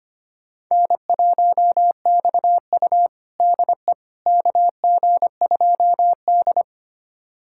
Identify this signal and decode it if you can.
Morse code, heard as a keyed tone, about 25 wpm, 700 Hz. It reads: N1XU DE KG2D
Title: USA Callsign DE Callsign - 4 Characters - 25wpm